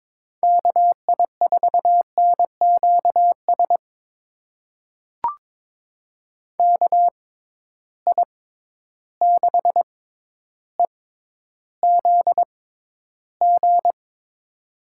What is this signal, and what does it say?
Morse code, heard as a keyed tone, about 22 wpm, spaced as for 11 wpm, 700 Hz. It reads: KI4NQS KI6EZG
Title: USA Call Signs - 6 Characters - 6x Character Spacing - 22wpm